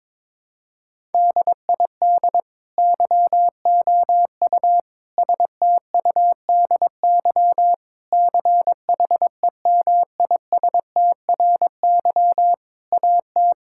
Morse code, heard as a keyed tone, about 22 wpm, 700 Hz. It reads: DID YOU STUDY CHEMISTRY AT
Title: Sentences from Top 1000 Words - Review Part 6 - 22wpm